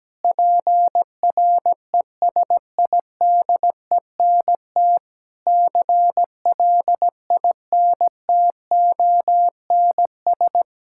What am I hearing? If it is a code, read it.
Morse code, heard as a keyed tone, about 17 wpm, 700 Hz. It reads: PRESIDENT CLINTONS
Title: Pangram Challenge v2 - 17wpm